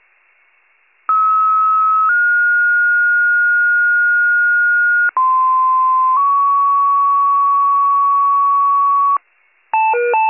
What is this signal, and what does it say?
Two-tone page, heard as a one-second tone, then a three-second tone, about 1290 then 1430 Hz, about 1040 then 1120 Hz. It is heard from 1.1 to 9.2 s.